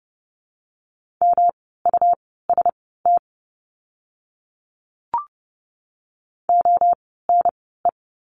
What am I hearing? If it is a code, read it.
Morse code, heard as a keyed tone, about 28 wpm, spaced as for 17 wpm, 700 Hz. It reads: MUST ONE
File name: course / 30:10wpm / 237 Lesson 38f - Letter q - Words 30wpm